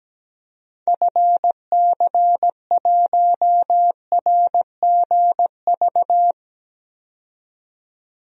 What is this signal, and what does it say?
Morse code, heard as a keyed tone, about 17 wpm, 700 Hz. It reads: FC1RGV
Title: DXCC-100 Call Signs - 6 Characters - 17wpm